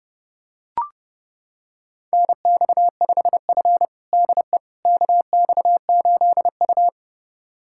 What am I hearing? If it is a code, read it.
Morse code, heard as a keyed tone, about 30 wpm, 700 Hz. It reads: NX5F DE KX8U
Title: USA Callsign DE Callsign - 4 Characters - 30wpm